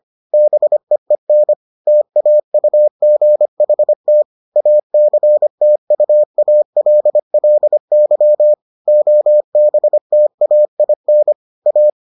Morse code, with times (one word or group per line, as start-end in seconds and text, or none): 0.3-1.5 BEEN
1.9-4.2 TAUGHT
4.6-8.5 ACTUALLY
8.9-11.3 OBTAIN
11.7-11.9 A